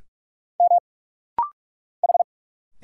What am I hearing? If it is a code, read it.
Morse code, heard as a keyed tone, about 46 wpm, 700 Hz. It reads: M H
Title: Single Letters - Rapid-Fire 45wpm